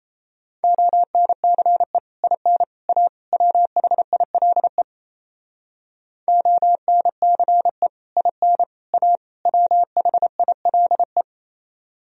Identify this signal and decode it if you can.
Morse code, heard as a keyed tone, about 30 wpm, 700 Hz. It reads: ONCE IN A WHILE ONCE IN A WHILE
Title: Speed-Racing - Sets of 4 Words 28wpm